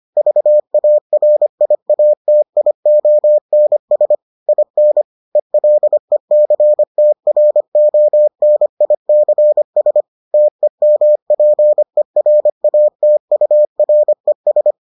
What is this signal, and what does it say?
Morse code, heard as a keyed tone, about 25 wpm, 600 Hz. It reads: VARIATIONS IN ELECTRONICS TEMPERATURES